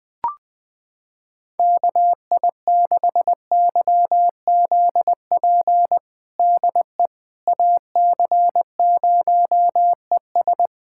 Morse code, heard as a keyed tone, about 20 wpm, 700 Hz. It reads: KI6YZP DE AC0ES
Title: USA Callsign DE Callsign 20wpm